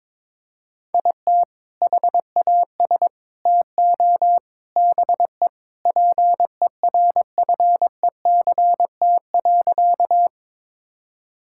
Morse code, heard as a keyed tone, about 22 wpm, 700 Hz. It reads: IT HAS TO BE PERFECT.
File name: practice / 22wpm / Sentences with 4-Word Phrases - 22wpm